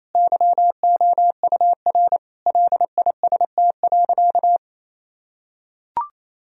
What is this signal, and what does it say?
Morse code, heard as a keyed tone, about 27 wpm, 700 Hz. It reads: YOUR LIST.
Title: Sentences from Top 1000 Words - Review Part 2 - 28wpm